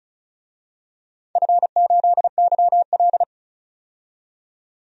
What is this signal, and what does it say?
Morse code, heard as a keyed tone, about 34 wpm, 700 Hz. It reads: F8YL
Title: DXCC-100 Call Signs - 4 Characters - 35wpm